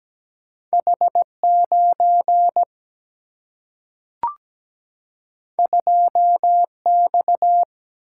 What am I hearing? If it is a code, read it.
Morse code, heard as a keyed tone, about 17 wpm, 700 Hz. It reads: H9 2X
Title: Single-Letter Single-Number 17wpm